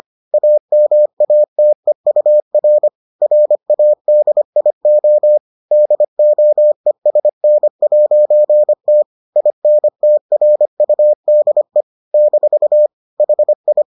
Morse code, heard as a keyed tone, about 25 wpm, 600 Hz. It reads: AMATEUR RADIO DOESN'T INTRUDE - HI